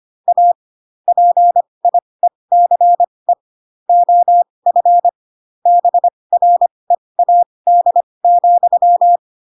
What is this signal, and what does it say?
Morse code, heard as a keyed tone, about 25 wpm, spaced as for 20 wpm, 700 Hz. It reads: A PIECE OF BREAD,